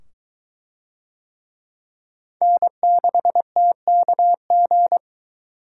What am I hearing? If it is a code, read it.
Morse code, heard as a keyed tone, about 21 wpm, 700 Hz. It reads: N6TKG